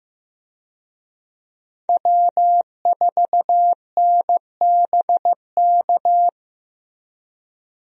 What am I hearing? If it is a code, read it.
Morse code, heard as a keyed tone, about 16 wpm, 700 Hz. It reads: W4NBK
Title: USA Call Signs - 5 Characters - Repeated 2x - 15wpm